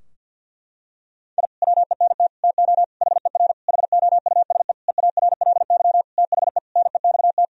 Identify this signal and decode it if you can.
Morse code, heard as a keyed tone, about 46 wpm, 700 Hz. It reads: I WENT TO HER HOUSE EARLY THE NEXT
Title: Sentences from Top 1000 Words - Review Part 2 - 50wpm